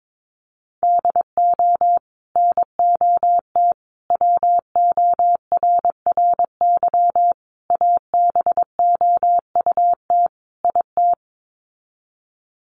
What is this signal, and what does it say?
Morse code, heard as a keyed tone, about 22 wpm, 700 Hz. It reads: DO NOT WORRY ABOUT IT
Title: Sets of 5 Words 22wpm